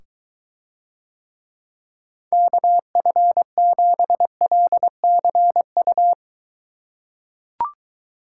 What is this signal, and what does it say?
Morse code, heard as a keyed tone, about 22 wpm, 700 Hz. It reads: KF7LCU